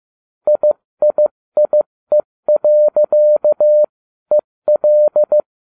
Morse code, heard as a keyed tone, about 15 wpm, spaced as for 13 wpm, 600 Hz. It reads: IIIE.EL